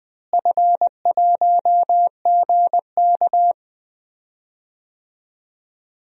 Morse code, heard as a keyed tone, about 20 wpm, 700 Hz. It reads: F1GK
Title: DXCC-100 Call Signs - 4 Characters - Repeated 3x - 20wpm